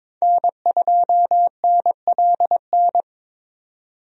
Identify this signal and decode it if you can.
Morse code, heard as a keyed tone, about 22 wpm, 700 Hz. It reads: N2NLN
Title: USA Call Signs - 5 Characters - Repeated 2x - 22wpm